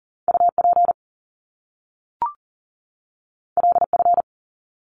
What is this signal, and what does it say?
Morse code, heard as a keyed tone, about 41 wpm, 700 Hz. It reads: UP LF
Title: Top Two Letter Combinations in Words 40wpm